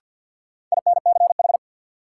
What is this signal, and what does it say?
Morse code, heard as a keyed tone, about 47 wpm, 700 Hz. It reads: INCH
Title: Sentences from Top 1000 Words - Review Part 3 - New Word Drill - 50wpm